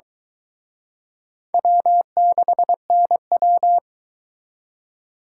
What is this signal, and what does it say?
Morse code, heard as a keyed tone, about 24 wpm, 700 Hz. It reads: W6NW